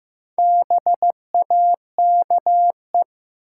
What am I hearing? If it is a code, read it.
Morse code, heard as a keyed tone, about 15 wpm, 700 Hz. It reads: BAKE
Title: Four Letter Words - 15wpm